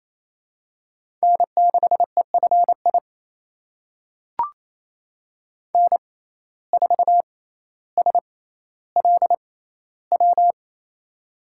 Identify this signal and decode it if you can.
Morse code, heard as a keyed tone, about 27 wpm, spaced as for 14 wpm, 700 Hz. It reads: N6EFI N4SLW
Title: USA Call Signs - 5 Characters - 6x Character Spacing - 28wpm